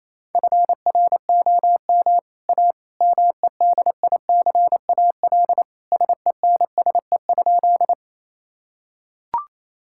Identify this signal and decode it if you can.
Morse code, heard as a keyed tone, about 28 wpm, 700 Hz. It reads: FROM A MEDICAL SENSE?
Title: Sentences from Top 1000 Words - Review Part 5 - 28wpm